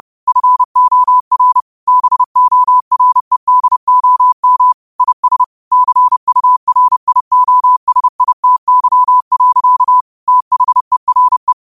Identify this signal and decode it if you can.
Morse code, heard as a keyed tone, about 30 wpm, 1000 Hz. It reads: FOR BOREDOM IS CURIOSITY. THERE
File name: qotdfiles / 30wpm